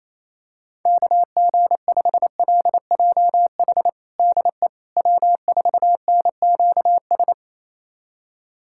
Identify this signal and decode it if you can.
Morse code, heard as a keyed tone, about 28 wpm, 700 Hz. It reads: KG5LJH DE W4NQS
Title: USA Callsign DE Callsign 28wpm